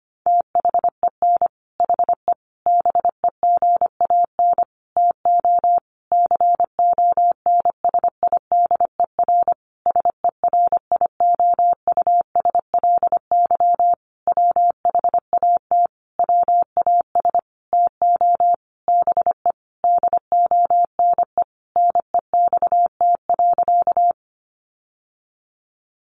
Morse code, heard as a keyed tone, about 25 wpm, 700 Hz. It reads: THEN HE BEGAN TO CONSIDER SERIOUSLY WHAT WAS TO BE DONE NEXT.
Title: Sentences from Top 1200 Words 25wpm